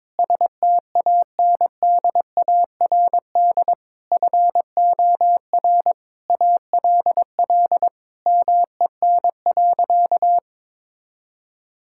Morse code, heard as a keyed tone, about 22 wpm, 700 Hz. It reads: STANDARD FOR ALL MEN.
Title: Sentences from Top 1000 Words - Review Part 6 - 22wpm